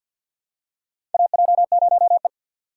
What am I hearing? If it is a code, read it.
Morse code, heard as a keyed tone, about 50 wpm, 700 Hz. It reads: AJ0E
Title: USA Call Signs - 3x Character Spacing - 50wpm